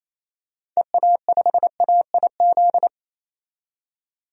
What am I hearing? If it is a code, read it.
Morse code, heard as a keyed tone, about 28 wpm, 700 Hz. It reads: EA5AIZ
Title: DXCC-100 Call Signs - 6 Characters - 28wpm